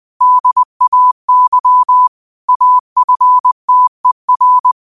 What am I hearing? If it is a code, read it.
Morse code, heard as a keyed tone, about 20 wpm, 1000 Hz. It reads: DAY AFTER